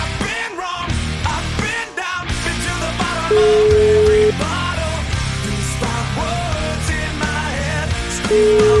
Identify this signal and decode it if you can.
Ringback tone (Europe, 425 Hz), with noise.